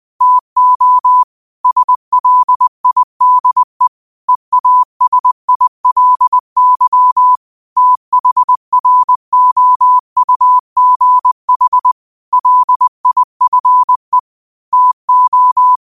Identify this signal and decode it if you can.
Morse code, heard as a keyed tone, about 20 wpm, 1000 Hz. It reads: TO SLIDE EASILY THROUGH LIFE TO
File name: qotdfiles / 20wpm